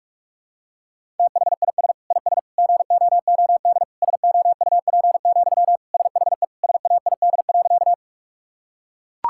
Morse code, heard as a keyed tone, about 41 wpm, 700 Hz. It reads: THIS IS GOOD SOUP, SHE SAID.
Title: Sentences from Top 1000 Words - Review Part 6 - 45wpm